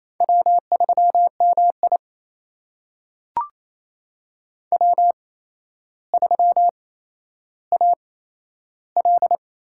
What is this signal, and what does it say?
Morse code, heard as a keyed tone, about 28 wpm, spaced as for 12 wpm, 700 Hz. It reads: W3MI W3AL
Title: USA Call Signs - 4 Characters - 8x Character Spacing - 28wpm